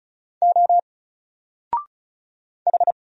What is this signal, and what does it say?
Morse code, heard as a keyed tone, about 34 wpm, 700 Hz. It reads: O H
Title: Single Letter-Number Alt 35wpm